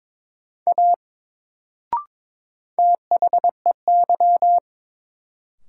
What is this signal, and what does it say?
Morse code, heard as a keyed tone, about 22 wpm, 700 Hz. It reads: A THEY
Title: Sentences from Top 100 Words - New Word Drill - 22wpm